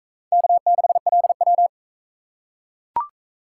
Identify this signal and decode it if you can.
Morse code, heard as a keyed tone, about 44 wpm, 700 Hz. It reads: K6LW